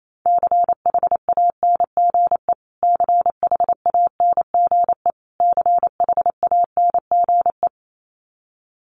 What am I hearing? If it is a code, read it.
Morse code, heard as a keyed tone, about 28 wpm, 700 Hz. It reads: CHANGE CHANGE CHANGE
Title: Top 300 Words Repeated 28wpm